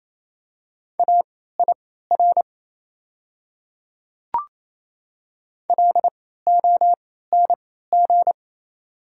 Morse code, heard as a keyed tone, about 28 wpm, spaced as for 16 wpm, 690 Hz. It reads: AIR LONG
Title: 3x Character Spacing - Top 500 Words - 28wpm